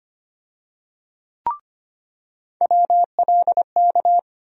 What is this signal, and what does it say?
Morse code, heard as a keyed tone, about 25 wpm, 700 Hz. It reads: WLK
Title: USA Call Sign Postfixes - 3 Letters 25wpm